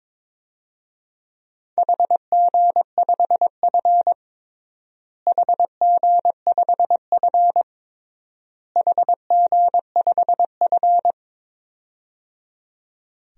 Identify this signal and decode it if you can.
Morse code, heard as a keyed tone, about 22 wpm, 700 Hz. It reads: HG5F HG5F HG5F
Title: DXCC-100 Call Signs - 4 Characters - Repeated 3x - 22wpm